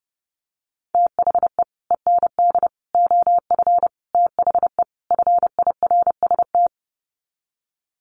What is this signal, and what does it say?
Morse code, heard as a keyed tone, about 30 wpm, 700 Hz. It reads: THE END OF THE FIRST
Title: Sets of 5 Words Encore 30wpm